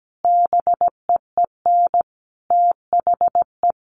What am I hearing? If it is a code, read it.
Morse code, heard as a keyed tone, about 17 wpm, 700 Hz. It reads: BEEN THE
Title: Sentences from Top 1400 Words 17wpm